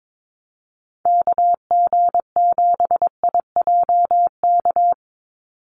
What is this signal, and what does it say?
Morse code, heard as a keyed tone, about 22 wpm, 700 Hz. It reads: KG7IJK
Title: USA Call Signs 22wpm